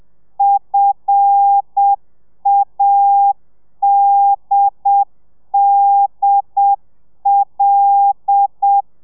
Morse code, heard as a keyed tone, about 7 wpm, 800 Hz. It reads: FADDL
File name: telegrafia / i2tfj / Lez5 30